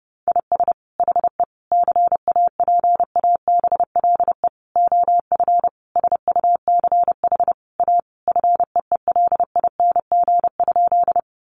Morse code, heard as a keyed tone, about 30 wpm, 700 Hz. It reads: IS HE CAPABLE OF SUCH A FEELING?
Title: Sentences from Top 1200 Words 30wpm